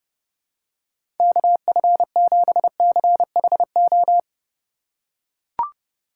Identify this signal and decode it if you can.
Morse code, heard as a keyed tone, about 29 wpm, 700 Hz. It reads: KF7CHO
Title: USA Call Signs - 6 Characters - 4x Character Spacing - 30wpm